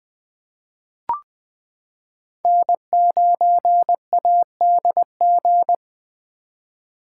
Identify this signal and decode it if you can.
Morse code, heard as a keyed tone, about 20 wpm, 700 Hz. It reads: N9ADG
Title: K1USN SST QSO Practice - 20wpm